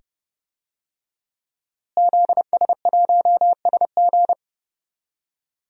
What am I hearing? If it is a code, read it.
Morse code, heard as a keyed tone, about 31 wpm, 700 Hz. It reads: ZS1SG
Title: DXCC-100 Call Signs - 5 Characters - Repeated 2x - 30wpm